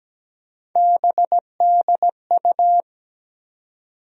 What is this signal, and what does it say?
Morse code, heard as a keyed tone, about 17 wpm, 700 Hz. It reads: BDU
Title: USA Call Sign Postfixes - 3 Letters 17wpm